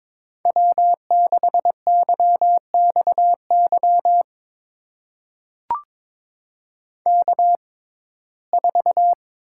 Morse code, heard as a keyed tone, about 22 wpm, spaced as for 16 wpm, 700 Hz. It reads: W6YXY K4
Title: USA Call Signs - 5 Characters - 6x Character Spacing - 22wpm